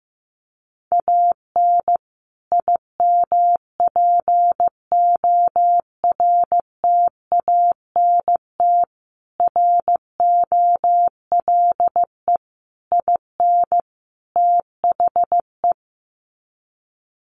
Morse code, heard as a keyed tone, about 15 wpm, 700 Hz. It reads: AN IMPORTANT ROLE IN THE
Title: Sets of 5 Words Encore2 15wpm